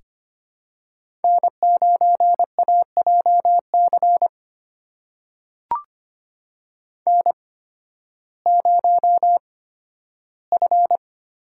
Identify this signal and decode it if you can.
Morse code, heard as a keyed tone, about 25 wpm, spaced as for 13 wpm, 700 Hz. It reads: N9AJC N0F
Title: USA Call Signs - 5 Characters - 8x Character Spacing - 25wpm